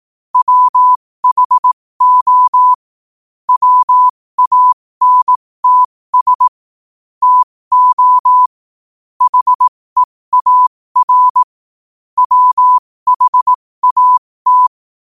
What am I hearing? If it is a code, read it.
Morse code, heard as a keyed tone, about 18 wpm, spaced as for 15 wpm, 1000 Hz. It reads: WHO WANTS TO HEAR WHAT
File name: qotdfiles / 15wpm